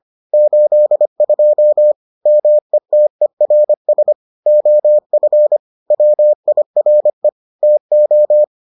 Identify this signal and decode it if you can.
Morse code, heard as a keyed tone, about 25 wpm, 600 Hz. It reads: 82 METERS OF WIRE TO